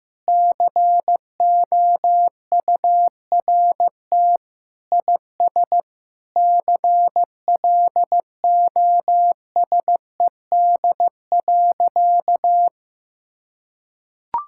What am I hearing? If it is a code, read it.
Morse code, heard as a keyed tone, about 15 wpm, 700 Hz. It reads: COURT IS CLOSED.
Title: Sentences from Top 1600 Words 15wpm